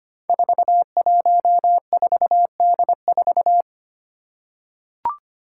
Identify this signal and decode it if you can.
Morse code, heard as a keyed tone, about 25 wpm, 700 Hz. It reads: 414D4
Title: Five Challenging Letter-Number Combinations 25wpm